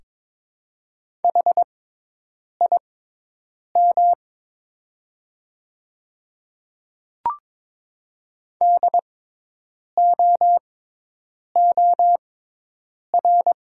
Morse code, heard as a keyed tone, about 22 wpm, spaced as for 8 wpm, 700 Hz. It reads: HIM DOOR
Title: 6x Character Spacing - Top 300 Words - 22wpm